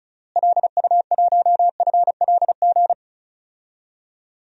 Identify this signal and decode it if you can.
Morse code, heard as a keyed tone, about 36 wpm, 700 Hz. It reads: LU1FLG
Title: DXCC-100 Call Signs - 6 Characters - Repeated 3x - 35wpm